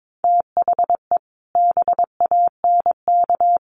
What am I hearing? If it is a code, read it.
Morse code, heard as a keyed tone, about 22 wpm, 700 Hz. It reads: THE BANK
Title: Sentences from Top 1500 Words 22wpm